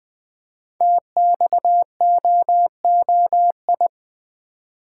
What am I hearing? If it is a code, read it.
Morse code, heard as a keyed tone, about 20 wpm, 700 Hz. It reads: TXOOI
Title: Code Groups - 5 Letters - 20wpm